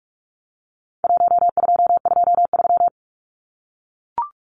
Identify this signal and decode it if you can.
Morse code, heard as a keyed tone, about 44 wpm, 700 Hz. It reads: 1223
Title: Numbers - Contest Style - 4-Digit 45wpm